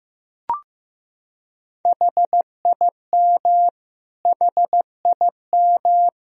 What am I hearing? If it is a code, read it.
Morse code, heard as a keyed tone, about 15 wpm, 700 Hz. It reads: HIM HIM
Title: Top 100 Words Repeated 15wpm